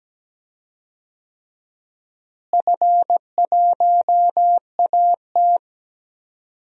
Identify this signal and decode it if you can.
Morse code, heard as a keyed tone, about 17 wpm, 700 Hz. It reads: F1AT
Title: DXCC-100 Call Signs - 4 Characters - Repeated 3x - 17wpm